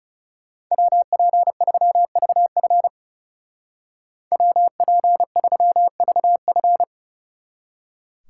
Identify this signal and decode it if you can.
Morse code, heard as a keyed tone, about 35 wpm, 700 Hz. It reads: WP3VF WP3VF
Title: Speed Racing - USA 2x2 Call Signs 30wpm